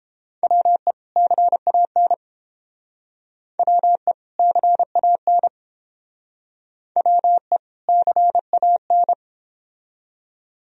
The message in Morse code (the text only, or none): WE CAN WE CAN WE CAN